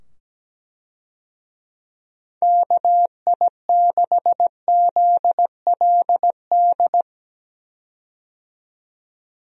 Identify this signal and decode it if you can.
Morse code, heard as a keyed tone, about 17 wpm, 700 Hz. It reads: KI6ZLD
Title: USA Call Signs - 6 Characters - 17wpm